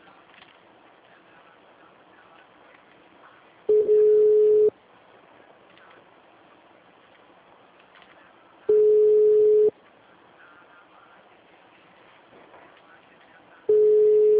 Ringback tone (Europe, 425 Hz).